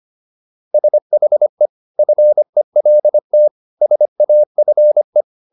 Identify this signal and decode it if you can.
Morse code, heard as a keyed tone, about 25 wpm, 600 Hz. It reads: SHE FELT SAFE